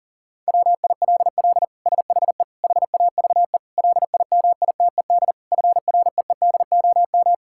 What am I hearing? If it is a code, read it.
Morse code, heard as a keyed tone, about 40 wpm, 700 Hz. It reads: WILL SHE HAVE LIMITED FREEDOM
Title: Sentences from Top 1000 Words - Review Part 6 - 40wpm